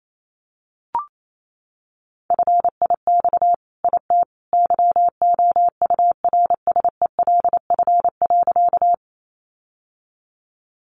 Morse code, heard as a keyed tone, about 28 wpm, 700 Hz. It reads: FIX IT YOURSELF.
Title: Sentences from Top 1400 Words 28wpm